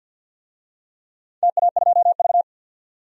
Morse code, heard as a keyed tone, about 51 wpm, 700 Hz. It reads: TA2V
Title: DXCC-100 Call Signs - 4 Characters - 50wpm